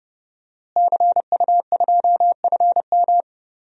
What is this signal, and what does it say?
Morse code, heard as a keyed tone, about 30 wpm, 700 Hz. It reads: CU2FM
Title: DXCC-100 Call Signs - 5 Characters - Repeated 3x - 30wpm